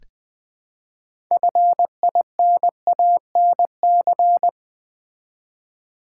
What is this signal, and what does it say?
Morse code, heard as a keyed tone, about 20 wpm, 700 Hz. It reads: FINANC
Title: Build A Word - 3 Letters At A Time - 9-Letter Words - 20wpm